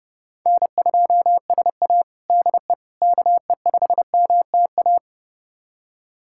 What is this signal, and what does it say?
Morse code, heard as a keyed tone, about 32 wpm, 700 Hz. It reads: N2SA DE KE5MTA
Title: USA Callsign DE Callsign 30wpm